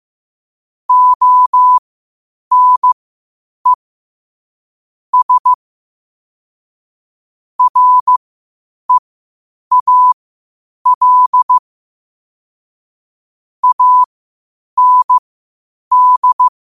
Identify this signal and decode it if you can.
Morse code, heard as a keyed tone, about 15 wpm, spaced as for 8 wpm, 1000 Hz. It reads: ONE S REAL AND
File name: qotdfiles / 8wpm